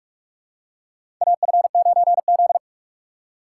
Morse code, heard as a keyed tone, about 44 wpm, 700 Hz. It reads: AF9Z
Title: USA Call Signs - 4 Characters - 6x Character Spacing - 45wpm